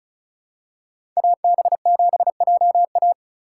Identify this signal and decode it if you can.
Morse code, heard as a keyed tone, about 34 wpm, 700 Hz. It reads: AB7JA
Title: USA Call Signs - 5 Characters - 3x Character Spacing - 35wpm